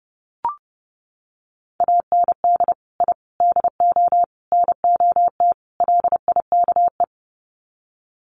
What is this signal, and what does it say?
Morse code, heard as a keyed tone, about 30 wpm, 700 Hz. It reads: AND I DO NOT LIKE
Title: Sets of 5 Words Encore2 30wpm